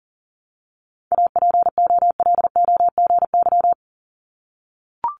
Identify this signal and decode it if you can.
Morse code, heard as a keyed tone, about 40 wpm, 700 Hz. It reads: APOLOGY
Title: Multi-letter Phonemes Repeated 3x 40wpm